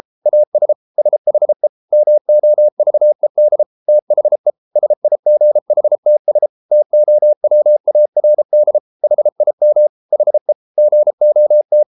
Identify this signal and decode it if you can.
Morse code, heard as a keyed tone, about 33 wpm, 600 Hz. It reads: AS SHE MOVED THE SIGHTS TOWARD HIM HE GOT